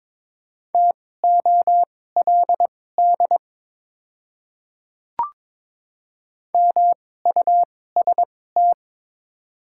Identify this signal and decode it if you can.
Morse code, heard as a keyed tone, about 22 wpm, spaced as for 16 wpm, 700 Hz. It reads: TOLD MUST
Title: 2x Character Spacing - Top 500 Words - 22wpm